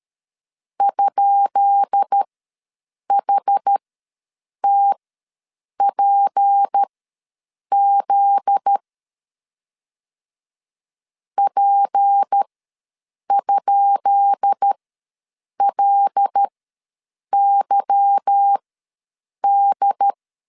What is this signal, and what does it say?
Morse code, heard as a keyed tone, about 13 wpm, spaced as for 7 wpm, 800 Hz. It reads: ?HTPZ P?LYD
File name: LEZIONI / 10 SIGNS / 10B INT ws 12wpm 3527